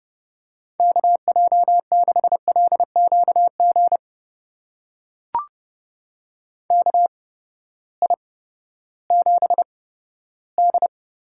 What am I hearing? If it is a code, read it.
Morse code, heard as a keyed tone, about 30 wpm, spaced as for 15 wpm, 700 Hz. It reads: KJ6LQG KI7D
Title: USA Call Signs - 6 Characters - 8x Character Spacing - 30wpm